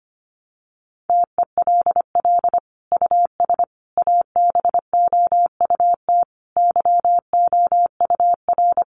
Morse code, heard as a keyed tone, about 25 wpm, 700 Hz. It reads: TELL US ABOUT YOUR
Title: Sentences from Top 1300 Words 25wpm